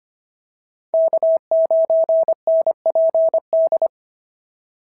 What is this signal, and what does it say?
Morse code, heard as a keyed tone, about 25 wpm, 650 Hz. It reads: K9NPD
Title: SOTA QSO Practice 25wpm